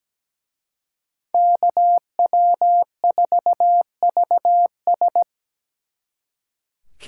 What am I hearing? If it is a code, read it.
Morse code, heard as a keyed tone, about 20 wpm, 700 Hz. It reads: KW4VS
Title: Speed Racing - USA 2x2 Call Signs 17wpm